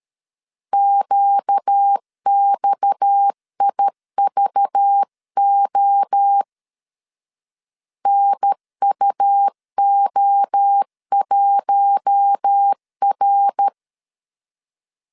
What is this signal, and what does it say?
Morse code, heard as a keyed tone, about 13 wpm, 800 Hz. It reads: QXIVO NUO1R
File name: CW / DET / 1 6 1500C S B 12wpm 3431